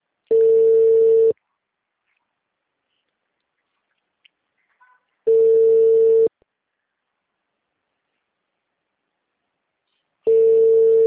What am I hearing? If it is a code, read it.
Ringback tone (China, 450 Hz).